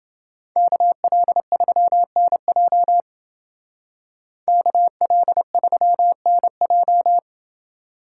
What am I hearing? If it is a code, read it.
Morse code, heard as a keyed tone, about 28 wpm, 700 Hz. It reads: KL3NJ KL3NJ